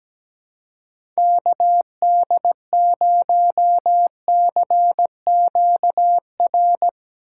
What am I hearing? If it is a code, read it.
Morse code, heard as a keyed tone, about 17 wpm, 700 Hz. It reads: KD0CQR
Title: USA Call Signs - 6 Characters - 3x Character Spacing - 17wpm